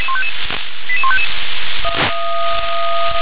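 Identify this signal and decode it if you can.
Five-tone selective call (ZVEI), heard as quick runs of five tones, tones of about 70 ms. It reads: 88155
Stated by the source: ZVEI five-tone selective call received by SDR.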